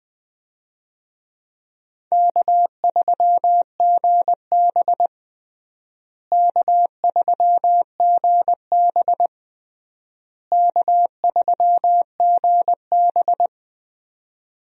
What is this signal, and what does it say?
Morse code, heard as a keyed tone, about 20 wpm, 700 Hz. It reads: K3GB K3GB K3GB
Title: USA Call Signs - 4 Characters - Repeated 3x - 20wpm